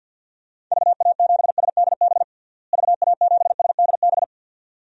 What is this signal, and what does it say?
Morse code, heard as a keyed tone, about 47 wpm, 700 Hz. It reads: VA7SDB VA7SDB
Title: DXCC-100 Call Signs - 6 Characters - Repeated 2x - 50wpm